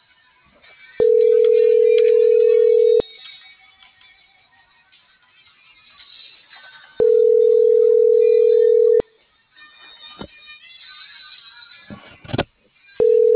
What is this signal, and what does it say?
Ringback tone (North America), 440 + 480 Hz.